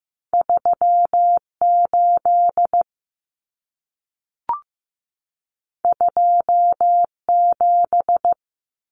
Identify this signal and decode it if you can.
Morse code, heard as a keyed tone, about 15 wpm, 700 Hz. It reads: 38 27
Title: Numbers - 2-Digit 15wpm